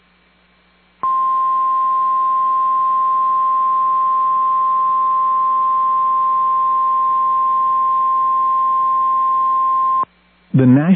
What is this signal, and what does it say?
Emergency Alert System (SAME) tones (the 1050 Hz attention tone), with speech. It is heard from 1.0 to 10.1 s.